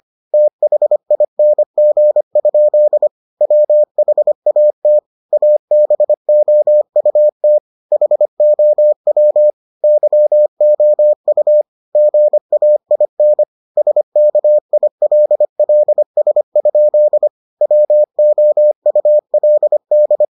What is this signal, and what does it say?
Morse code, heard as a keyed tone, about 25 wpm, 600 Hz. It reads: THING? WHAT ABOUT HOW YOU GAIN SKILLS? WOULD